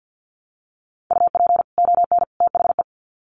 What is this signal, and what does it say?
Morse code, heard as a keyed tone, about 50 wpm, 700 Hz. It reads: UP ON THE
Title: Sets of 3 Words 50wpm